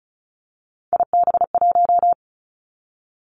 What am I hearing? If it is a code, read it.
Morse code, heard as a keyed tone, about 36 wpm, 700 Hz. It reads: IB1
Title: Three Challenging Letter-Number Combinations 35wpm